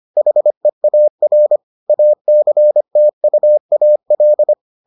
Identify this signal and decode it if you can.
Morse code, heard as a keyed tone, about 25 wpm, 600 Hz. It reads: HEAR ACTUAL